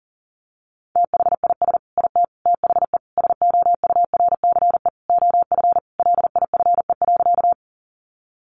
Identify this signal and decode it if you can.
Morse code, heard as a keyed tone, about 40 wpm, 700 Hz. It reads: THIS IT THE SOURCE OF LIFE.
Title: Sentences from Top 800 Words 40wpm